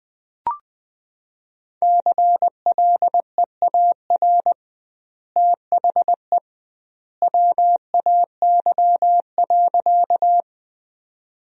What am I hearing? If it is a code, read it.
Morse code, heard as a keyed tone, about 20 wpm, 700 Hz. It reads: CLEAR THE WAY.
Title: 2x Word-Spacing - Sentences from Top 500 Words - 20wpm